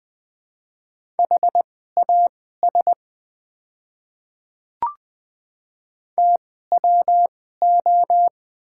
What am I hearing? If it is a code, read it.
Morse code, heard as a keyed tone, about 21 wpm, spaced as for 15 wpm, 700 Hz. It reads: HAS TWO
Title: 2x Character Spacing - Top 100 Words - 20wpm